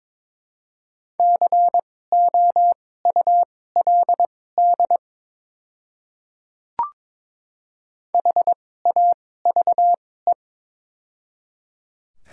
Morse code, heard as a keyed tone, about 22 wpm, spaced as for 16 wpm, 700 Hz. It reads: COULD HAVE
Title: 2x Character Spacing - Top 200 Words - 22wpm